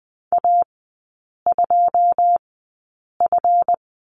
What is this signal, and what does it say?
Morse code, heard as a keyed tone, about 20 wpm, spaced as for 8 wpm, 700 Hz. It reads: A2F
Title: 159 Lesson 26e - Letter g - Three Characters 20wpm